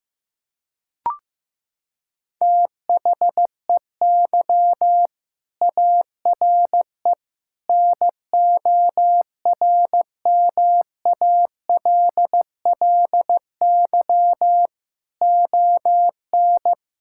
Morse code, heard as a keyed tone, about 15 wpm, 700 Hz. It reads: THEY ARE NORMALLY ON
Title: Sentences from Top 1000 Words - Review Part 6 - 15wpm